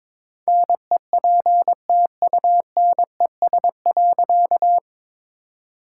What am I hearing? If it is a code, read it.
Morse code, heard as a keyed tone, about 22 wpm, 700 Hz. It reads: NEPTUNES.
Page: morse-code-ninja-practice-22wpm